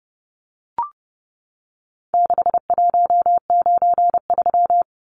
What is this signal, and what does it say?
Morse code, heard as a keyed tone, about 30 wpm, 700 Hz. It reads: 6193
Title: Numbers - 4-Digit 30wpm